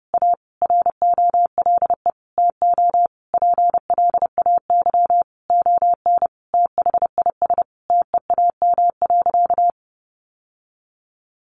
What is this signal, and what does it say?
Morse code, heard as a keyed tone, about 30 wpm, 700 Hz. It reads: A ROLE TO PLAY ON THIS TEAM.
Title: Sentences from Top 1500 Words 30wpm